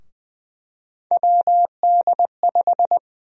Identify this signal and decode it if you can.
Morse code, heard as a keyed tone, about 21 wpm, 700 Hz. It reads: WD5
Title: USA Call Sign Prefixes - 20wpm